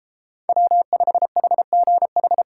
Morse code, heard as a keyed tone, about 33 wpm, 700 Hz. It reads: W5HGH